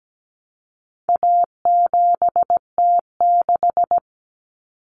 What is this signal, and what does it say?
Morse code, heard as a keyed tone, about 17 wpm, 700 Hz. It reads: A7T6
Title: Common Cut Numbers - 4-Digit 17wpm